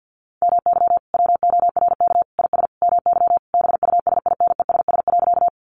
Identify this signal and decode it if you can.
Morse code, heard as a keyed tone, about 50 wpm, 700 Hz. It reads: MY WORK IS MY BUSINESS.
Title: Sentences from Top 600 Words 50wpm